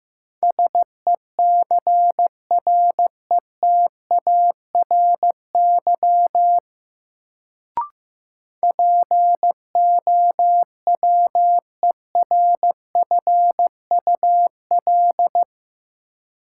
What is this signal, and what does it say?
Morse code, heard as a keyed tone, about 15 wpm, 700 Hz. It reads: SECRETARY POWERFUL
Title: Sentences from Top 1300 Words - New Word Drill Alt - 15wpm